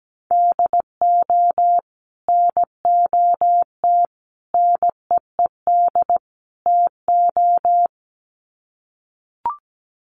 Morse code, heard as a keyed tone, about 17 wpm, 700 Hz. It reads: DO NOT NEED TO
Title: Sets of 4 Words 17wpm